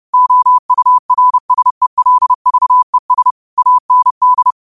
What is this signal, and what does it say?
Morse code, heard as a keyed tone, about 30 wpm, 1000 Hz. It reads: OURSELVES AND